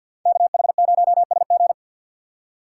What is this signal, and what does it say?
Morse code, heard as a keyed tone, about 64 wpm, 700 Hz. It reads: KH0SG